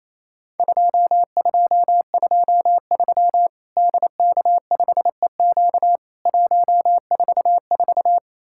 Morse code, heard as a keyed tone, about 28 wpm, 700 Hz. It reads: 2223 DK5EQ 144
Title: Worked All Europe DX Contest - QTC Exchange - 28wpm